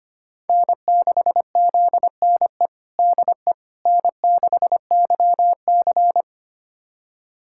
Morse code, heard as a keyed tone, about 25 wpm, 700 Hz. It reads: N6ZNE DE N6YC
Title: USA Callsign DE Callsign 25wpm